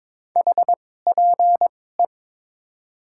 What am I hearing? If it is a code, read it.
Morse code, heard as a keyed tone, about 21 wpm, spaced as for 16 wpm, 700 Hz. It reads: HPE
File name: Top 90 QSO Elements - 2x Character Spacing - 22wpm